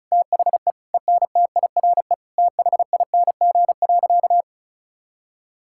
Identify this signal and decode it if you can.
Morse code, heard as a keyed tone, about 34 wpm, 700 Hz. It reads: THE ENTIRE THING.
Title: Sentences from Top 1000 Words - Review Part 5 - 35wpm